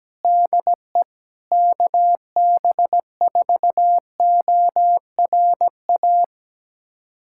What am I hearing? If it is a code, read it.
Morse code, heard as a keyed tone, about 17 wpm, 700 Hz. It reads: DE KB4ORA
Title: USA Callsign DE Callsign - 6 Characters - 17wpm